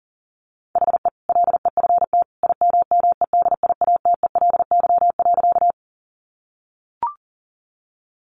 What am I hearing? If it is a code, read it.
Morse code, heard as a keyed tone, about 37 wpm, 700 Hz. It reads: HE LEFT IMMEDIATELY.
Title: Sentences from Top 1100 Words 40wpm